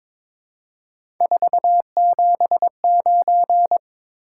Morse code, heard as a keyed tone, about 22 wpm, 700 Hz. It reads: 479
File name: practice / 22wpm / Three Letter-Number Combinations - 22wpm